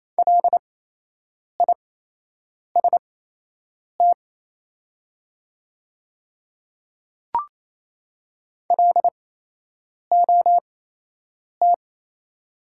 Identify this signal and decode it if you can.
Morse code, heard as a keyed tone, about 28 wpm, spaced as for 8 wpm, 700 Hz. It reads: LIST LOT
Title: 8x Character Spacing - Top 500 Words - 28wpm